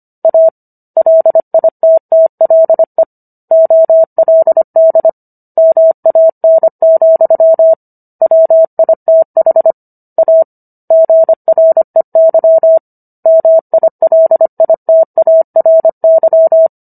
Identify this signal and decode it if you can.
Morse code, heard as a keyed tone, about 25 wpm, 650 Hz. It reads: A LITTLE OLD MAN, WITH A GREY MILITARY